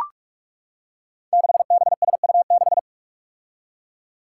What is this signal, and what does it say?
Morse code, heard as a keyed tone, about 44 wpm, 700 Hz. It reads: 6BSU6
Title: Five Challenging Letter-Number Combinations 45wpm